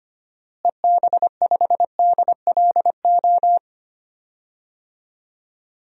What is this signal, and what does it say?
Morse code, heard as a keyed tone, about 25 wpm, 700 Hz. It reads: EB5DLO
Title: DXCC-100 Call Signs - 6 Characters - Repeated 2x - 25wpm